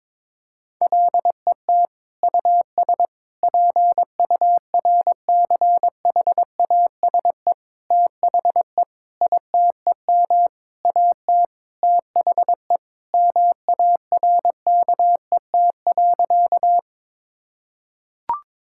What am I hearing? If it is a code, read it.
Morse code, heard as a keyed tone, about 22 wpm, 700 Hz. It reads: LET US PURCHASE THE ITEM AT THE MARKET.
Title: Sentences from Top 1000 Words - Review Part 6 - 22wpm